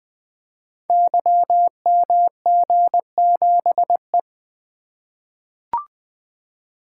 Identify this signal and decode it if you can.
Morse code, heard as a keyed tone, about 21 wpm, 700 Hz. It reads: YMG7E
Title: Code Groups - 5 Letter:Numbers - 20wpm